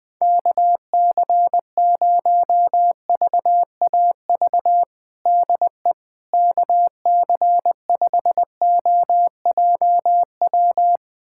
Morse code, heard as a keyed tone, about 20 wpm, 700 Hz. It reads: KC0VAV DE KC5OJW
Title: USA Callsign DE Callsign 20wpm